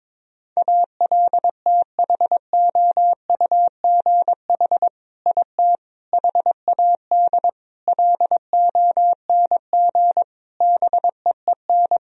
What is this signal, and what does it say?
Morse code, heard as a keyed tone, about 22 wpm, 700 Hz. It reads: ALTHOUGH IT HAD LONG BEEN
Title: Sentences from Top 1000 Words - Review Part 3 - 22wpm